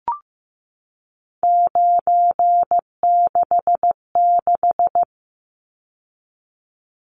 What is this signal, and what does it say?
Morse code, heard as a keyed tone, about 15 wpm, 700 Hz. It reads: 966
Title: Numbers - 3-Digit 15wpm